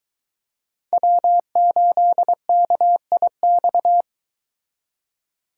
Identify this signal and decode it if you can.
Morse code, heard as a keyed tone, about 23 wpm, 700 Hz. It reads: W8KIX